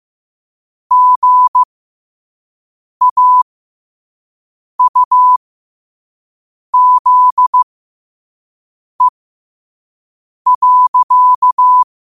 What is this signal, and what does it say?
Morse code, heard as a keyed tone, about 15 wpm, spaced as for 5 wpm, 1000 Hz. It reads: GAUZE.
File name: qotdfiles / 5wpm